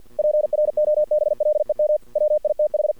Morse code, heard as a keyed tone, about 50 wpm, 600 Hz. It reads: PROBLEM WITH
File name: archived / CRSnet